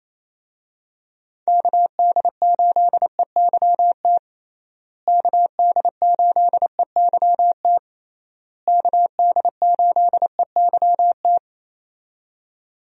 Morse code, heard as a keyed tone, about 28 wpm, 700 Hz. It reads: KD8EYT KD8EYT KD8EYT
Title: USA Call Signs - 6 Characters - Repeated 3x - 28wpm